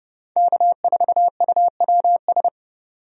Morse code, heard as a keyed tone, about 30 wpm, 700 Hz. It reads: K4UWS
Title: USA Call Signs - 5 Characters - 30wpm